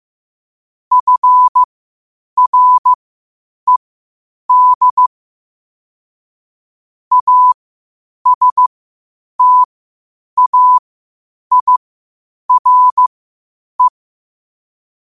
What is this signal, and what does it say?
Morse code, heard as a keyed tone, about 15 wpm, spaced as for 8 wpm, 1000 Hz. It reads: FRED ASTAIRE
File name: qotdfiles / 8wpm